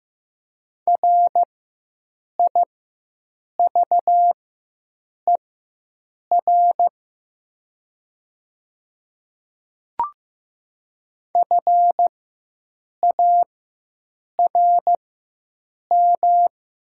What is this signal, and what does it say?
Morse code, heard as a keyed tone, about 15 wpm, spaced as for 7 wpm, 700 Hz. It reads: RIVER FARM
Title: 4x Character Spacing - Top 500 Words - 15wpm